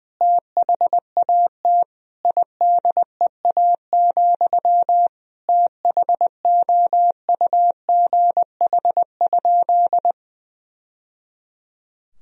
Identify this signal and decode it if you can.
Morse code, heard as a keyed tone, about 21 wpm, 700 Hz. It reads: THAT IDEA, THOUGH?
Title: Sentences from Top 1000 Words - Review Part 2 - 20wpm